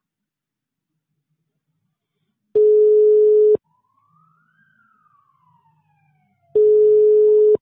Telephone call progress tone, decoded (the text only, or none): ringback tone